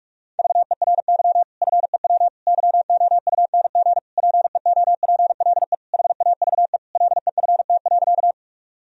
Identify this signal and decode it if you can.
Morse code, heard as a keyed tone, about 42 wpm, 700 Hz. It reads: VERY FEW YOUNG PEOPLE HAVE LEFT.
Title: Sentences from Top 1000 Words - Review Part 1 - 45wpm